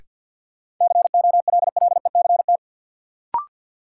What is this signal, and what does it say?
Morse code, heard as a keyed tone, about 53 wpm, 700 Hz. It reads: COLLECT
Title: Sentences from Top 1600 Words - New Word Drill - 50wpm